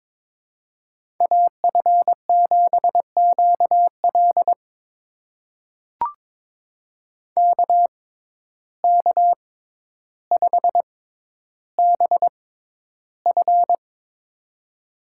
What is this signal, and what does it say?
Morse code, heard as a keyed tone, about 22 wpm, spaced as for 11 wpm, 700 Hz. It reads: AF7QL KK5BF
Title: USA Call Signs - 5 Characters - 6x Character Spacing - 22wpm